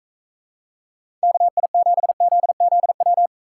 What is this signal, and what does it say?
Morse code, heard as a keyed tone, about 42 wpm, 700 Hz. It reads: KI7ZZW